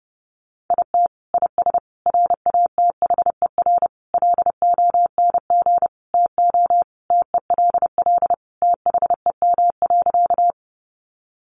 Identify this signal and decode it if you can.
Morse code, heard as a keyed tone, about 30 wpm, 700 Hz. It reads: IT IS RATHER LONG TO TELL THEM.
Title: Sentences from Top 700 Words 30wpm